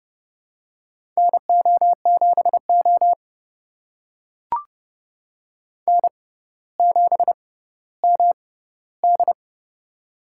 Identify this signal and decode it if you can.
Morse code, heard as a keyed tone, about 30 wpm, spaced as for 15 wpm, 700 Hz. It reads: NO7O N7MD
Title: USA Call Signs - 4 Characters - 6x Character Spacing - 30wpm